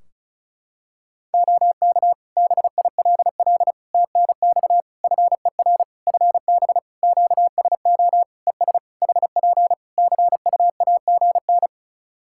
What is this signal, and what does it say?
Morse code, heard as a keyed tone, about 36 wpm, 700 Hz. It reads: OK BILL TNX FER FB QSO ES HP CUAGN
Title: Practice QSO - Exchange 4 - Ending - 35wpm